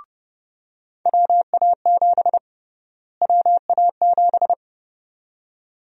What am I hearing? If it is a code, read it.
Morse code, heard as a keyed tone, about 30 wpm, 700 Hz. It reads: WA7 WA7
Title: USA Call Sign Prefixes - Repeated 2x - 30wpm